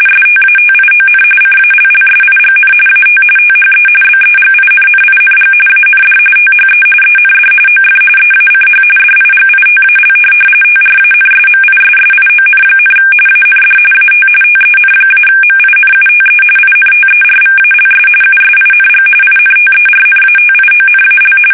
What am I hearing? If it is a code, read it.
RTTY, about 45.45 baud. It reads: STATION CALLSIGN AT THE BEGINNING AND THE END OF EACH DIGITAL TRANSMISSION, AND AT TEN-MINUTE INTERVALS USING INTERNATIONAL
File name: fsk850 / teletype-RTTY